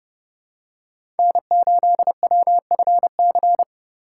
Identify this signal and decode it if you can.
Morse code, heard as a keyed tone, about 30 wpm, 700 Hz. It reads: N8WFC
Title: USA Call Signs - 5 Characters - 4x Character Spacing - 30wpm